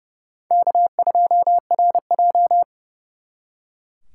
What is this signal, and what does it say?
Morse code, heard as a keyed tone, about 38 wpm, 700 Hz. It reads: K2RJ